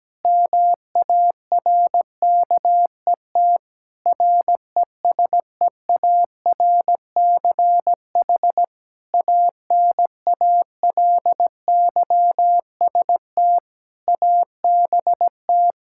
Morse code, heard as a keyed tone, about 17 wpm, 700 Hz. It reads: MARKET RESEARCH ANALYST ABT
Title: Practice QSO - Exchange 3 - More Info - 17wpm